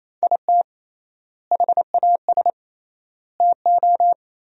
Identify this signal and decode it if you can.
Morse code, heard as a keyed tone, about 27 wpm, 700 Hz. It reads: IT HAS TO
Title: 3x Word-Spacing - Sentences from Top 500 Words - 28wpm